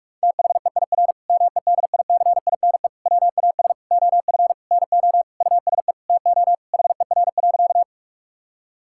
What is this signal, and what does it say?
Morse code, heard as a keyed tone, about 46 wpm, 700 Hz. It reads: THEIR MEDICINE WAS OF NO USE TO HER.
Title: Sentences from Top 1000 Words - Review Part 6 - 45wpm